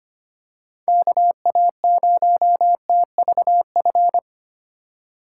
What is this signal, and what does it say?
Morse code, heard as a keyed tone, about 25 wpm, 700 Hz. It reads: KA0TVF
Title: USA Call Signs - 6 Characters - 3x Character Spacing - 25wpm